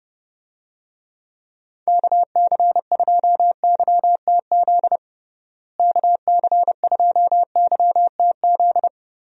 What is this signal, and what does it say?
Morse code, heard as a keyed tone, about 32 wpm, 700 Hz. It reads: KC2YTZ KC2YTZ
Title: USA Call Signs - 6 Characters - Repeated 2x - 30wpm